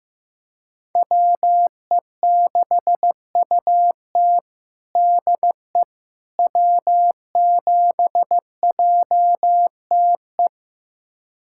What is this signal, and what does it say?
Morse code, heard as a keyed tone, about 15 wpm, 700 Hz. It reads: WE6UT DE W7JTE
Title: USA Callsign DE Callsign - 5 Characters - 15wpm